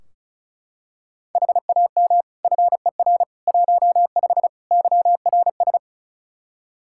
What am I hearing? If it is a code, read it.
Morse code, heard as a keyed tone, about 36 wpm, 700 Hz. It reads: HAM FER 15 YRS
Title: K1USN SST QSO Practice 35wpm